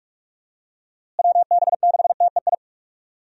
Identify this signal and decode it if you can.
Morse code, heard as a keyed tone, about 46 wpm, 700 Hz. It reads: WB6TEI
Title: USA Call Signs - 6 Characters - Repeated 2x - 45wpm